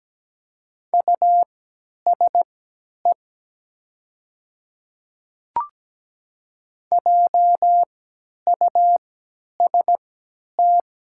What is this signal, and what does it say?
Morse code, heard as a keyed tone, about 17 wpm, spaced as for 10 wpm, 690 Hz. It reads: USE JUST
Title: 3x Character Spacing - Top 200 Words - 17wpm